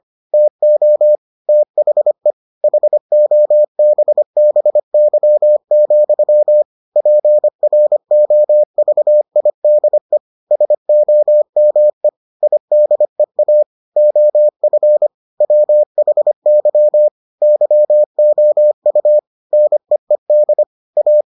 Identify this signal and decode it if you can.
Morse code, heard as a keyed tone, about 25 wpm, 600 Hz. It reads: TO THE HOBBY, PROVIDE SOME IDEA OF WHY YOU NEED A